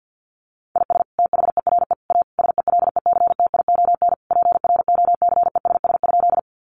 Morse code, heard as a keyed tone, about 50 wpm, 700 Hz. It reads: IS THERE A SELECTION PROCESS?
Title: Sentences from Top 1400 Words 50wpm